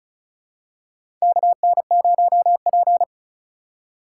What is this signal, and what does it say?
Morse code, heard as a keyed tone, about 36 wpm, 700 Hz. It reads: KN0P
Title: USA Call Signs - 4 Characters - 6x Character Spacing - 35wpm